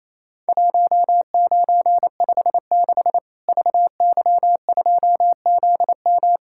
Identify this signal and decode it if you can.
Morse code, heard as a keyed tone, about 27 wpm, 700 Hz. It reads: 1956 VY2ZM
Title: Worked All Europe DX Contest - QTC Exchange - 28wpm